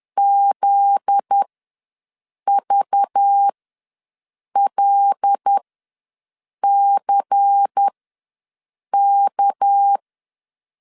Morse code, heard as a keyed tone, about 10 wpm, spaced as for 6 wpm, 800 Hz. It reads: ZVLCK
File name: LEZIONI / 8 YJ45 / 8 YJ ws 10wpm 4043